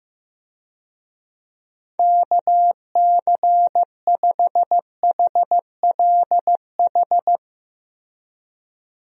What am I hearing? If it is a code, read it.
Morse code, heard as a keyed tone, about 15 wpm, 700 Hz. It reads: KC5HLH
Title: USA Call Signs - 6 Characters - Repeated 2x - 15wpm